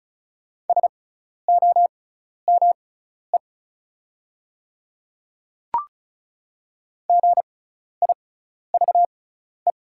Morse code, heard as a keyed tone, about 36 wpm, spaced as for 12 wpm, 700 Hz. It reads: SOME GIVE
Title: 6x Character Spacing - Top 200 Words - 35wpm